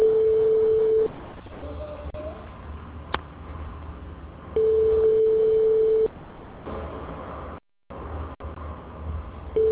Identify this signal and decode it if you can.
Ringback tone (France), 440 Hz.